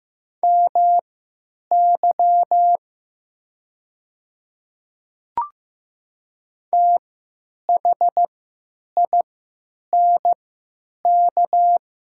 Morse code, heard as a keyed tone, about 15 wpm, spaced as for 8 wpm, 700 Hz. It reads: MY THINK
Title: 3x Character Spacing - Top 300 Words - 15wpm